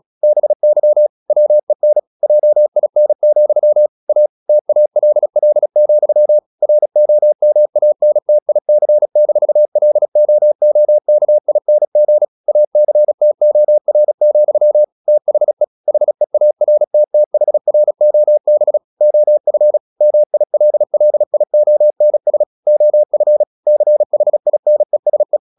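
Morse code, heard as a keyed tone, about 36 wpm, 600 Hz. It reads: BY WEN JIN, A TALL, ROMANTIC-LOOKING ACTOR, THE HEARTTHROB OF MILLIONS OF CHINESE
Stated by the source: Morse practice at 36 wpm, text of Ken Follett's novel Never, from CW med Gnister.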